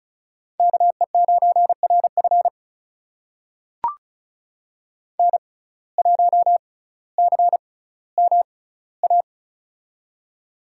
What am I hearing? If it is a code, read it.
Morse code, heard as a keyed tone, about 36 wpm, spaced as for 18 wpm, 700 Hz. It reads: KE9RF N1CMA
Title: USA Call Signs - 6x Character Spacing - 35wpm